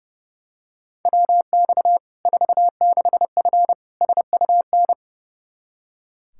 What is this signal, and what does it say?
Morse code, heard as a keyed tone, about 30 wpm, 700 Hz. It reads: WX 46F SUN
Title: Practice QSO - Exchange 2 - Info - 30wpm